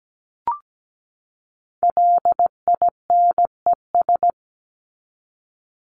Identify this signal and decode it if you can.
Morse code, heard as a keyed tone, about 17 wpm, 700 Hz. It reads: LINES
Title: ES:S Suffix - Top 100 Words with Base Words 17wpm